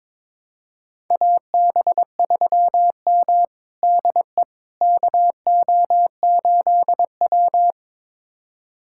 Morse code, heard as a keyed tone, about 21 wpm, 700 Hz. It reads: AB3M DE KO8W
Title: USA Callsign DE Callsign - 4 Characters - 22wpm